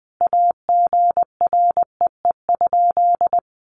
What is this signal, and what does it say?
Morse code, heard as a keyed tone, about 20 wpm, 700 Hz. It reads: AGREE?
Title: Sentences from Top 1100 Words 20wpm